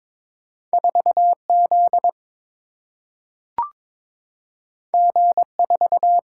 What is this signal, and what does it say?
Morse code, heard as a keyed tone, about 21 wpm, 700 Hz. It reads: 4 Z G 4
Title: Single-Letter Single-Number 22wpm